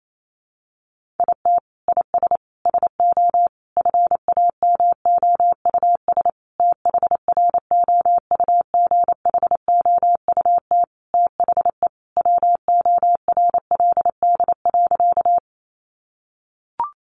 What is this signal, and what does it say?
Morse code, heard as a keyed tone, about 28 wpm, 700 Hz. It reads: IT IS SO FAMOUS THROUGHOUT THE WORLD.
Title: Sentences from Top 1500 Words 28wpm